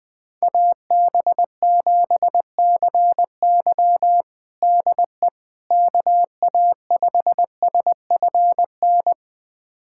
Morse code, heard as a keyed tone, about 20 wpm, 700 Hz. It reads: AB7CY DE KA5SFN
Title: USA Callsign DE Callsign 20wpm